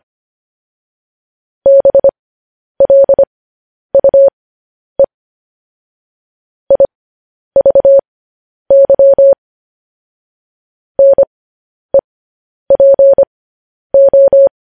Morse code, heard as a keyed tone, about 25 wpm, spaced as for 10 wpm, 550 Hz. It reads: BLUE IVY NEPO
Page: News Headlines in Morse Code at 10 WPM